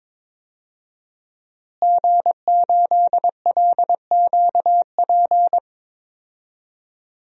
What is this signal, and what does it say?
Morse code, heard as a keyed tone, about 22 wpm, 700 Hz. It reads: G8LQP
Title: DXCC-100 Call Signs - 5 Characters - Repeated 2x - 22wpm